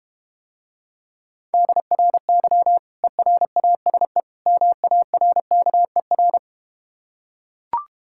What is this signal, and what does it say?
Morse code, heard as a keyed tone, about 31 wpm, 700 Hz. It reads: DRY ERASE MARKER